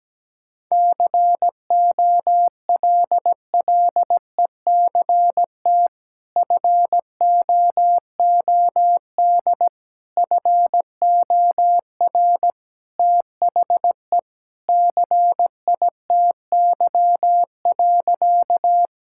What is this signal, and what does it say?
Morse code, heard as a keyed tone, about 17 wpm, 700 Hz. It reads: COLLECT FOOD FOR THE CITY.
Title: Sentences from Top 1600 Words 17wpm